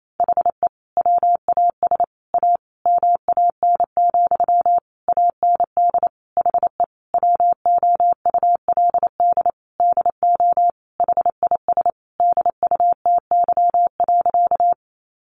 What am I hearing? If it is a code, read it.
Morse code, heard as a keyed tone, about 28 wpm, 700 Hz. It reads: HE WAS A MAN, AND HE WOULD DO HIS DUTY.
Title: Sentences from Top 1500 Words 28wpm